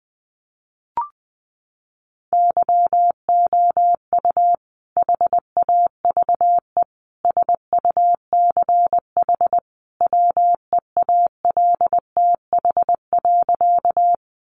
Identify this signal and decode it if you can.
Morse code, heard as a keyed tone, about 20 wpm, 700 Hz. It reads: YOU HAVE SUCH WEALTH.
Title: Sentences from Top 1200 Words 20wpm